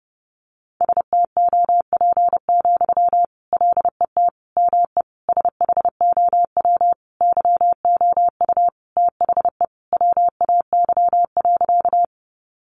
Morse code, heard as a keyed tone, about 30 wpm, 700 Hz. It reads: STOP, LET ME SHOW YOU THE WAY.
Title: Sentences from Top 300 Words 30wpm